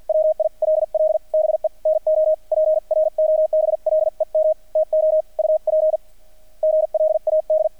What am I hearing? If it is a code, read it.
Morse code, heard as a keyed tone, about 50 wpm, spaced as for 38 wpm, 630 Hz, with radio noise. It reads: WILL BE NO PROBLEM TO UPGLAD
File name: archived / CRSnet